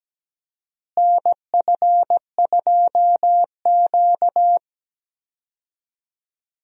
Morse code, heard as a keyed tone, about 17 wpm, 700 Hz. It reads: NF2Q
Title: USA Call Signs - 4 Characters - Repeated 3x - 17wpm